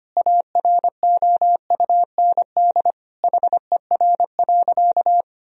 Morse code, heard as a keyed tone, about 25 wpm, 700 Hz. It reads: AROUND HER.
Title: Sentences from Top 1000 Words - Review Part 4 - 25wpm